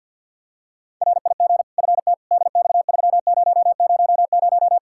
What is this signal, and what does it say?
Morse code, heard as a keyed tone, about 48 wpm, 700 Hz. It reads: RIG FT DX3000
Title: Practice QSO - Exchange 2 - Info - 50wpm